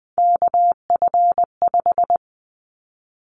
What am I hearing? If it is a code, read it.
Morse code, heard as a keyed tone, about 20 wpm, 700 Hz. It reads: KF5
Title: Callsign Prefixes for Top 25 DXCC - 20wpm